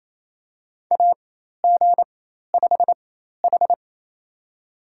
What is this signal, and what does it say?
Morse code, heard as a keyed tone, about 28 wpm, spaced as for 13 wpm, 700 Hz. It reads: AG5H
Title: USA Call Signs - 4 Characters - 4x Character Spacing - 28wpm